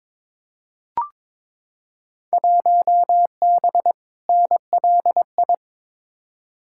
Morse code, heard as a keyed tone, about 22 wpm, 700 Hz. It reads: 1B NLI
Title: Field Day Exchanges - 22wpm